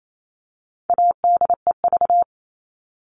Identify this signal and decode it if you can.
Morse code, heard as a keyed tone, about 28 wpm, 700 Hz. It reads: ADEV
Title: All Cut Numbers - 4-Digit 28wpm